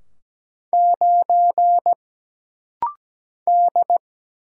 Morse code, heard as a keyed tone, about 17 wpm, 700 Hz. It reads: 9 D
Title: Single Letter-Number - Rapid-Fire 17wpm